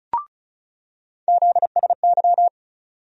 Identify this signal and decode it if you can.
Morse code, heard as a keyed tone, about 36 wpm, 700 Hz. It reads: ZSY
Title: USA Call Sign Postfixes - 3 Letters 35wpm